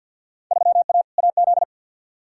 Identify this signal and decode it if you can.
Morse code, heard as a keyed tone, about 51 wpm, 700 Hz. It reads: 3A AZ
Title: Field Day Exchanges Encore - 50wpm